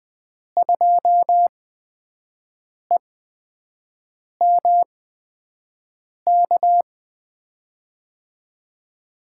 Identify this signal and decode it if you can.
Morse code, heard as a keyed tone, about 21 wpm, spaced as for 6 wpm, 700 Hz. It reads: 2EMK
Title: USA Call Signs - 6 Characters - 8x Character Spacing - 20wpm